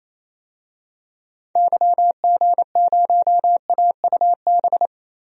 Morse code, heard as a keyed tone, about 28 wpm, 700 Hz. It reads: YG0AUB
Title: DXCC-100 Call Signs - 6 Characters - 28wpm